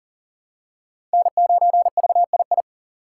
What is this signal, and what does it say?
Morse code, heard as a keyed tone, about 40 wpm, 700 Hz. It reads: N9VII
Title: USA Call Signs - 5 Characters - 2x Character Spacing - 40wpm